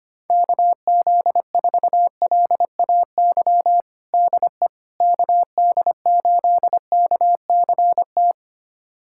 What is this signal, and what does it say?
Morse code, heard as a keyed tone, about 25 wpm, 700 Hz. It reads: KZ4LAY DE KD8KCT
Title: USA Callsign DE Callsign - 6 Characters - 25wpm